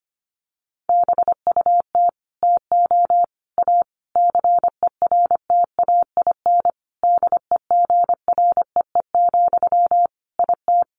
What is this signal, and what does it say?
Morse code, heard as a keyed tone, about 25 wpm, 700 Hz. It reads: BUT TO A CERTAIN DEGREE, IT
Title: Sentences from Top 1500 Words 25wpm